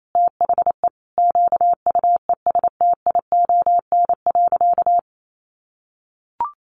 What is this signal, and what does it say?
Morse code, heard as a keyed tone, about 28 wpm, 700 Hz. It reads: THE QUESTION.
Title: Sentences from Top 400 Words 28wpm